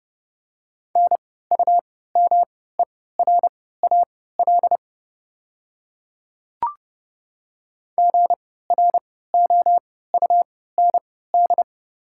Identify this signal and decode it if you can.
Morse code, heard as a keyed tone, about 30 wpm, spaced as for 17 wpm, 700 Hz. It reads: NUMERAL GROUND
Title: 3x Character Spacing - Top 500 Words - 30wpm